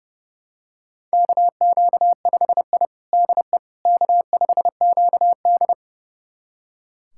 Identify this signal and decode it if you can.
Morse code, heard as a keyed tone, about 30 wpm, 700 Hz. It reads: KQ5I DE K5QD
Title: USA Callsign DE Callsign - 4 Characters - 30wpm